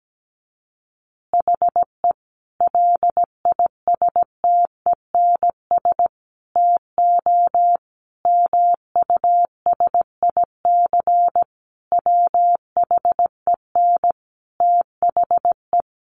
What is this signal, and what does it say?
Morse code, heard as a keyed tone, about 17 wpm, 700 Hz. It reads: HE LISTENS TO MUSIC WHEN THE
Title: Additional Easy Phrases 17wpm